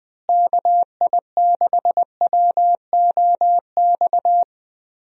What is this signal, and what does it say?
Morse code, heard as a keyed tone, about 20 wpm, 700 Hz. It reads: KI6WOX
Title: USA Call Signs - 6 Characters - Repeated 2x - 20wpm